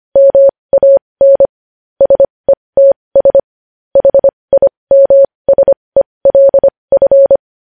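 Morse code, heard as a keyed tone, about 25 wpm, spaced as for 20 wpm, 550 Hz. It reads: MAN SETS HIMSELF
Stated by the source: Morse code news headlines practice at 20.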